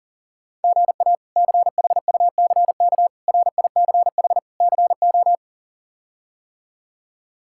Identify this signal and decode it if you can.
Morse code, heard as a keyed tone, about 40 wpm, 700 Hz. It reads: GA CHUCK RICH CO
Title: K1USN SST QSO Practice - 40wpm